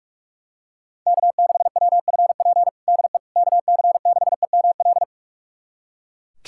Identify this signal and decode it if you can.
Morse code, heard as a keyed tone, about 41 wpm, 700 Hz. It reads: K6WFP DE KC6EML
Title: USA Callsign DE Callsign 45wpm